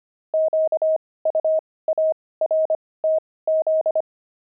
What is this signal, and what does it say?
Morse code, heard as a keyed tone, about 25 wpm, spaced as for 18 wpm, 630 Hz. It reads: QUARTZ
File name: pangram-fs-18-cs-25